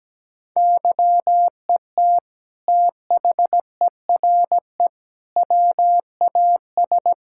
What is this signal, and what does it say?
Morse code, heard as a keyed tone, about 17 wpm, 700 Hz. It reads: YET THERE WAS
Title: Sentences from Top 1600 Words 17wpm